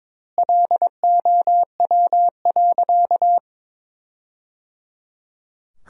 Morse code, heard as a keyed tone, about 22 wpm, 700 Hz. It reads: LOW.
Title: 2x Word-Spacing - Sentences from Top 500 Words - 22wpm